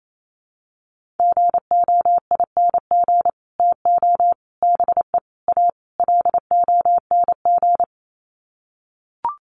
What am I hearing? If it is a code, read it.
Morse code, heard as a keyed tone, about 28 wpm, 700 Hz. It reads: GOING TO BE A LONG
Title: Sets of 5 Words Encore2 28wpm